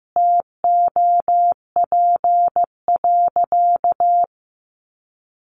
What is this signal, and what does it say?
Morse code, heard as a keyed tone, about 15 wpm, 700 Hz. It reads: TOP.
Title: Sentences from Top 1300 Words 15wpm